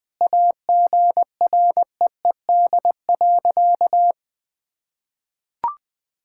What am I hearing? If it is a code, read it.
Morse code, heard as a keyed tone, about 20 wpm, 700 Hz. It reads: AGREED.
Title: Sentences with 8-Word Phrases - 20wpm